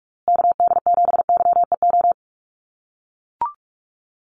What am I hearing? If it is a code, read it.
Morse code, heard as a keyed tone, about 44 wpm, 700 Hz. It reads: KD7YEO
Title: USA Call Signs Repeated 2x 45wpm